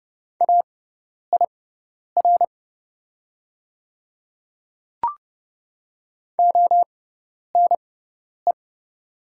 Morse code, heard as a keyed tone, about 30 wpm, spaced as for 10 wpm, 690 Hz. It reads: AIR ONE
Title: 6x Character Spacing - Top 200 Words - 30wpm